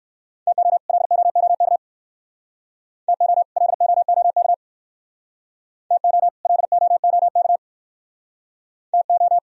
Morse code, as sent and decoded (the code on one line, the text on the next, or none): - --- / .-.. --- --- -.- / - --- / .-.. --- --- -.- / - --- / .-.. --- --- -.- / - ---
TO LOOK TO LOOK TO LOOK TO